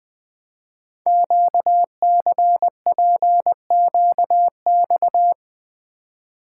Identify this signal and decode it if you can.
Morse code, heard as a keyed tone, about 20 wpm, 700 Hz. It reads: QCPQX
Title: Code Groups - 5 Letters - 20wpm